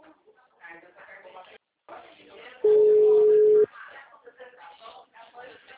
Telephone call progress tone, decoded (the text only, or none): ringback tone